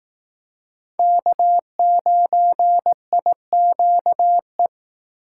Morse code, heard as a keyed tone, about 18 wpm, 700 Hz. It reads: K9IQE
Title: Speed Racing - USA 1x3 Call Signs 15wpm